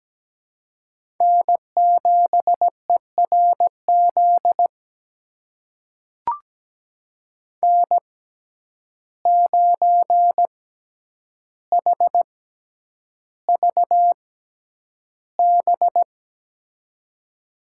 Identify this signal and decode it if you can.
Morse code, heard as a keyed tone, about 17 wpm, spaced as for 9 wpm, 700 Hz. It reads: N7ERZ N9HVB
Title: USA Call Signs - 6x Character Spacing - 17wpm